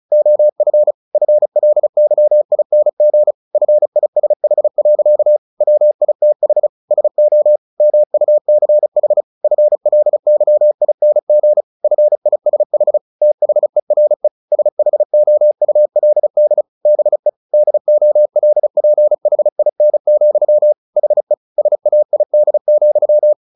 Morse code, heard as a keyed tone, about 34 wpm, 600 Hz. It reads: OF FLYING FISH. WITH SO MUCH FLYING FISH THERE SHOULD BE DOLPHIN, HE SAID,
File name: the old man and the sea 2 35wpm